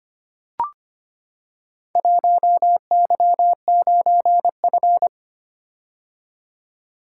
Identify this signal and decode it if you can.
Morse code, heard as a keyed tone, about 25 wpm, 700 Hz. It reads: 1Y9F
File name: practice / 25wpm / Four Letter-Number Combinations - 25wpm